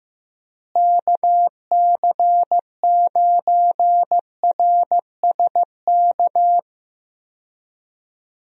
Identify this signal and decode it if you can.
Morse code, heard as a keyed tone, about 15 wpm, 700 Hz. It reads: KC9RSK
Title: USA Call Signs - 6 Characters - Repeated 2x - 15wpm